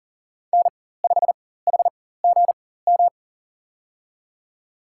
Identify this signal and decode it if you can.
Morse code, heard as a keyed tone, about 40 wpm, spaced as for 19 wpm, 700 Hz. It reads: N5HGM
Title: USA Call Signs - 4x Character Spacing - 40wpm